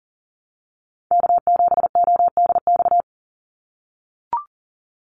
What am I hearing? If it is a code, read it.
Morse code, heard as a keyed tone, about 40 wpm, 700 Hz. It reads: K7ODX
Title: NA SOTA Activator Callsigns 40wpm